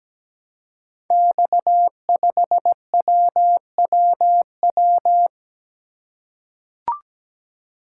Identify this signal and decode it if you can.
Morse code, heard as a keyed tone, about 17 wpm, 700 Hz. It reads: X5WWW
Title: Code Groups - 5 Letter:Numbers - 17wpm